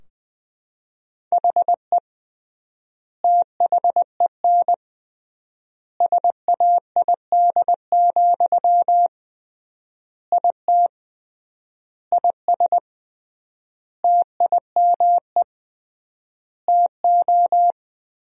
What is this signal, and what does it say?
Morse code, heard as a keyed tone, about 21 wpm, 700 Hz. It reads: HE THEN SAID, IT IS TIME TO
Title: 3x Word-Spacing - Sentences from Top 300 Words - 20wpm